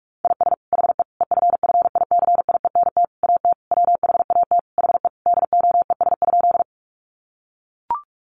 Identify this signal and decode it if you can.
Morse code, heard as a keyed tone, about 44 wpm, 700 Hz. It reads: IS HE EFFICIENT AT WHAT HE DOES?
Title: Sentences from Top 1300 Words 45wpm